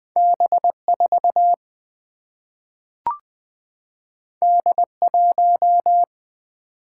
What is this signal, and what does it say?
Morse code, heard as a keyed tone, about 20 wpm, 700 Hz. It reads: B4 D1
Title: Single-Letter Single-Number 20wpm